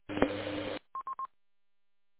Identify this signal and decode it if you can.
There is no code.